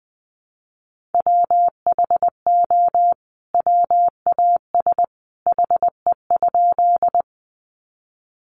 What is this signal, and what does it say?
Morse code, heard as a keyed tone, about 20 wpm, 700 Hz. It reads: WHO WAS HE?
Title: Sentences from Top 100 Words 20wpm